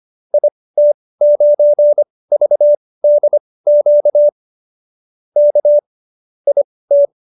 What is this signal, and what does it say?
Morse code, heard as a keyed tone, about 25 wpm, spaced as for 18 wpm, 590 Hz. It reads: IT9VDQ K IT
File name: qso215b 18